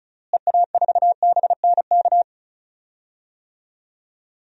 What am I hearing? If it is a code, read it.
Morse code, heard as a keyed tone, about 36 wpm, 700 Hz. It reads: EA4BNK
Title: DXCC-100 Call Signs - 6 Characters - Repeated 2x - 35wpm